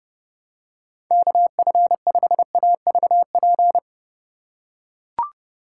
Morse code, heard as a keyed tone, about 30 wpm, 700 Hz. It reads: KF5AVP